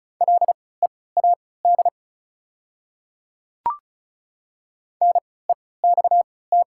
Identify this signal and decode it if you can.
Morse code, heard as a keyed tone, about 34 wpm, spaced as for 20 wpm, 690 Hz. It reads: LEAD NEXT
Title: 3x Character Spacing - Top 500 Words - 35wpm